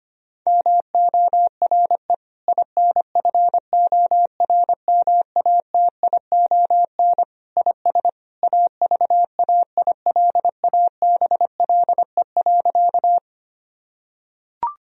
Morse code, heard as a keyed tone, about 25 wpm, 700 Hz. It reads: MORE INFORMATION IS AVAILABLE.
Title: Sentences from Top 1000 Words - Review Part 3 - 25wpm